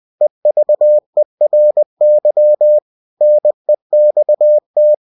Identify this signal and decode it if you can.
Morse code, heard as a keyed tone, about 20 wpm, 600 Hz. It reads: EVERY NEXT
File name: cw-books / winnie-the-pooh / winnie-20wpm-page-0096